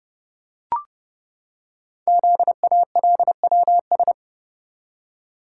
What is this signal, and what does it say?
Morse code, heard as a keyed tone, about 28 wpm, 700 Hz. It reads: ZALWS
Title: Code Groups - 5 Letters - 30wpm